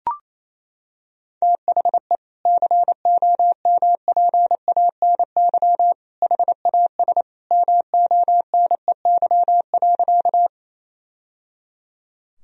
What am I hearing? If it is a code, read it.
Morse code, heard as a keyed tone, about 28 wpm, 700 Hz. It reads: THE COMPANY HAS MONEY.
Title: Sentences from Top 1000 Words - Review Part 3 - 28wpm